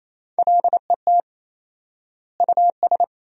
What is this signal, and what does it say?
Morse code, heard as a keyed tone, about 28 wpm, 700 Hz. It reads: LET US
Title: 4x Word-Spacing - Sentences from Top 400 Words - 28wpm